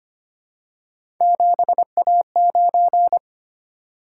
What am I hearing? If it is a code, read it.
Morse code, heard as a keyed tone, about 25 wpm, 700 Hz. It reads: 7A9
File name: Three Letter-Number Combinations - 25wpm